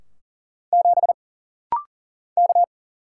Morse code, heard as a keyed tone, about 40 wpm, 700 Hz. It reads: 7 K
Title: Single Letter-Number Alt - Rapid-Fire 40wpm